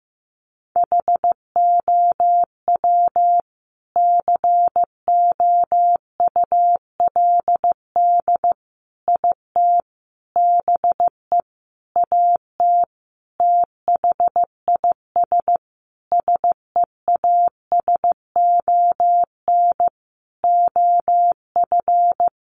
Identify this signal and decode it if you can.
Morse code, heard as a keyed tone, about 15 wpm, 700 Hz. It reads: HOW COULD IT BE AT THIS SEASON OF
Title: Sentences from Top 1400 Words 15wpm